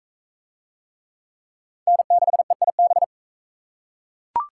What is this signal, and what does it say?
Morse code, heard as a keyed tone, about 36 wpm, 700 Hz. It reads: N6EIB